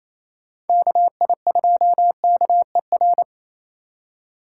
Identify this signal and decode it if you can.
Morse code, heard as a keyed tone, about 29 wpm, 700 Hz. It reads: KI2KER
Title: Speed Racing - USA Call Signs 28wpm